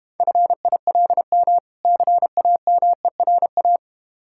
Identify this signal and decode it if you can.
Morse code, heard as a keyed tone, about 32 wpm, 700 Hz. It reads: FILM CAMERA